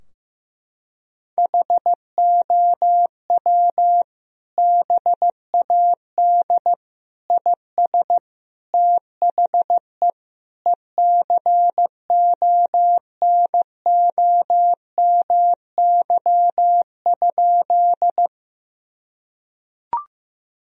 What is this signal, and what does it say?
Morse code, heard as a keyed tone, about 15 wpm, 700 Hz. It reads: HOW BAD IS THE ECONOMY?
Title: Sentences from Top 1000 Words - Review Part 3 - 15wpm